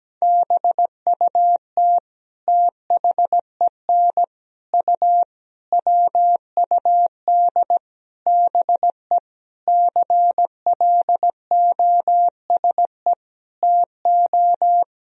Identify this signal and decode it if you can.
Morse code, heard as a keyed tone, about 17 wpm, 700 Hz. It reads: BUT THEN U WUD BE CLOSE TO
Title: Sentences from Top 300 Words and Abbreviations 17wpm